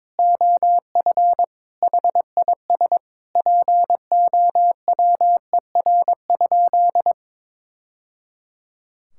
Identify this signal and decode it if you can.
Morse code, heard as a keyed tone, about 22 wpm, 700 Hz. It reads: OF HIS POWER?
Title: Sentences from Top 1000 Words - Review Part 5 - 22wpm